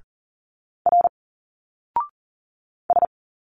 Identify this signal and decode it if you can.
Morse code, heard as a keyed tone, about 40 wpm, 700 Hz. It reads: R S
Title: Single Letters 40wpm